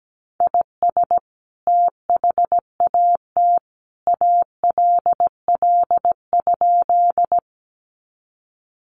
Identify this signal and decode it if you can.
Morse code, heard as a keyed tone, about 17 wpm, 700 Hz. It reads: IS THAT ALL?
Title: Sentences from Top 100 Words 17wpm